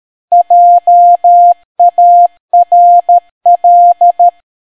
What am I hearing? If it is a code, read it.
Morse code, heard as a keyed tone, about 13 wpm, 700 Hz. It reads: JARL